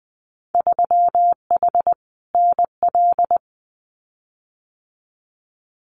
Morse code, heard as a keyed tone, about 20 wpm, 700 Hz. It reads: 3H NL
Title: Winter Field Day Exchanges Encore 20wpm